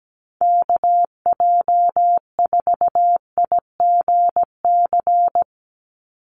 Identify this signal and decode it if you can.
Morse code, heard as a keyed tone, about 17 wpm, 700 Hz. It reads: KJ4IGC
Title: USA Call Signs 17wpm